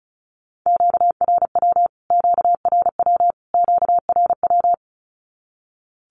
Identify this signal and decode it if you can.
Morse code, heard as a keyed tone, about 34 wpm, 700 Hz. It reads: QRW QRW QRW
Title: Q-Codes Repeated 3x 35wpm